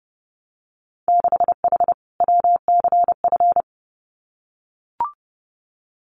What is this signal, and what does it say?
Morse code, heard as a keyed tone, about 30 wpm, 700 Hz. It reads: 6H WCF
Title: Winter Field Day Exchanges Encore 30wpm